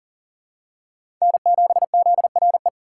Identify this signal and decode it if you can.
Morse code, heard as a keyed tone, about 40 wpm, 700 Hz. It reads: N7ZRE
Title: USA Call Signs - 5 Characters - 4x Character Spacing - 40wpm